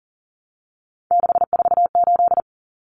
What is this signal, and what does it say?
Morse code, heard as a keyed tone, about 40 wpm, 700 Hz. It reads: 648
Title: Numbers - 3-Digit 40wpm